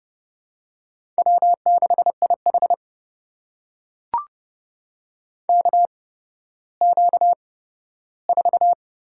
Morse code, heard as a keyed tone, about 30 wpm, spaced as for 15 wpm, 700 Hz. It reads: W6IH KQ4
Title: USA Call Signs - 4 Characters - 8x Character Spacing - 30wpm